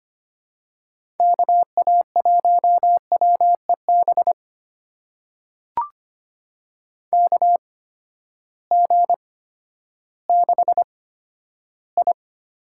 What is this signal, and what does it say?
Morse code, heard as a keyed tone, about 25 wpm, spaced as for 12 wpm, 700 Hz. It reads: KA1WEB KG6I
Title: USA Call Signs - 6 Characters - 8x Character Spacing - 25wpm